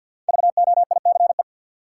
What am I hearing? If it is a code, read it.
Morse code, heard as a keyed tone, about 48 wpm, 700 Hz. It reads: VOICE
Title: Five Letter Words - 50wpm